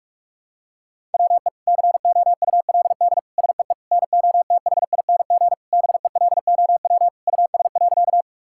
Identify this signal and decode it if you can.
Morse code, heard as a keyed tone, about 41 wpm, 700 Hz. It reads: WE COULD SEE NOTHING BELOW US.
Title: Sentences from Top 1000 Words - Review Part 5 - 45wpm